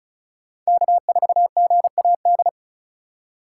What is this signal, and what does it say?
Morse code, heard as a keyed tone, about 34 wpm, 700 Hz. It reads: K4GAD
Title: USA Call Signs - 5 Characters - 4x Character Spacing - 35wpm